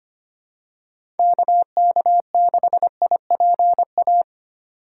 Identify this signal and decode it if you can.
Morse code, heard as a keyed tone, about 29 wpm, 700 Hz. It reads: KK6IPA
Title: Speed Racing - USA 2x3 Call Signs 25wpm